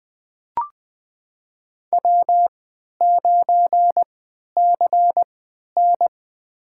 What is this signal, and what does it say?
Morse code, heard as a keyed tone, about 20 wpm, spaced as for 11 wpm, 700 Hz. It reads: W9CN
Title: USA Call Signs - 4 Characters - 3x Character Spacing - 20wpm